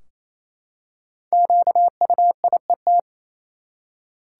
Morse code, heard as a keyed tone, about 28 wpm, 700 Hz. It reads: QUIET
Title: Five Letter Words - 28wpm